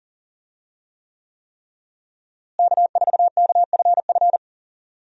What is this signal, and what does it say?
Morse code, heard as a keyed tone, about 40 wpm, 700 Hz. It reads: K4KFF
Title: USA Call Signs - 5 Characters - Repeated 3x - 40wpm